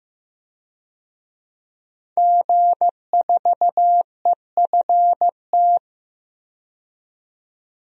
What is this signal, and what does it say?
Morse code, heard as a keyed tone, about 15 wpm, 700 Hz. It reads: G4EFT
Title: DXCC-100 Call Signs - 5 Characters - 15wpm